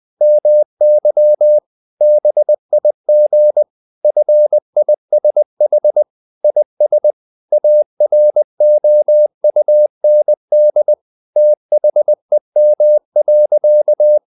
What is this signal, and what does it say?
Morse code, heard as a keyed tone, about 20 wpm, 600 Hz. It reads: MY BIG FISH IS AROUND THEM.
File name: the old man and the sea / the old man and the sea part 1 20wpm